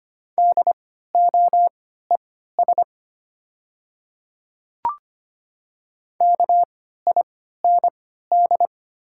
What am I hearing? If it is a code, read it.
Morse code, heard as a keyed tone, about 25 wpm, spaced as for 14 wpm, 700 Hz. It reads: DOES KIND
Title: 3x Character Spacing - Top 200 Words - 25wpm